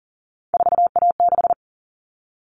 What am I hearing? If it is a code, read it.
Morse code, heard as a keyed tone, about 40 wpm, 700 Hz. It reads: 4A6
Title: All Cut Numbers - 3-Digit 40wpm